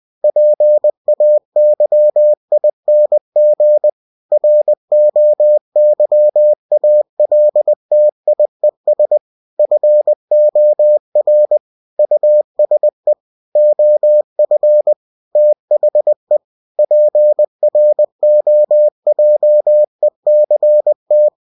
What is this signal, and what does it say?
Morse code, heard as a keyed tone, about 20 wpm, 600 Hz. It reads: PAYING ROYALTIES FOR USE OF THE PROJECT
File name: cw-books / winnie-the-pooh / winnie-20wpm-page-0112